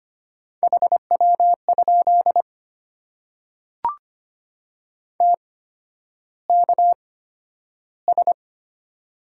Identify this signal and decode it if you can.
Morse code, heard as a keyed tone, about 25 wpm, spaced as for 11 wpm, 700 Hz. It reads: HW? TKS
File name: Top 90 QSO Elements - 8x Character Spacing - 25wpm